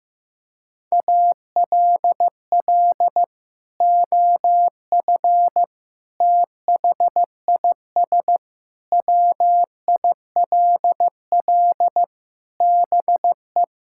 Morse code, heard as a keyed tone, about 16 wpm, 700 Hz. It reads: ALL OF THIS WILL BE
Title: Sentences from Top 1000 Words - Review Part 6 - 15wpm